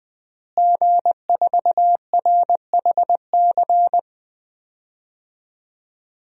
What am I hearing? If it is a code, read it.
Morse code, heard as a keyed tone, about 21 wpm, 700 Hz. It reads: G4RHC
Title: DXCC-100 Call Signs - 5 Characters - Repeated 3x - 20wpm